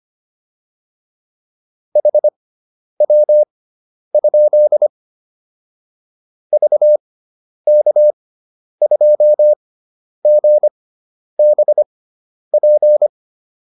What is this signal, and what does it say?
Morse code, heard as a keyed tone, about 25 wpm, spaced as for 10 wpm, 600 Hz. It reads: HW? VK2GBP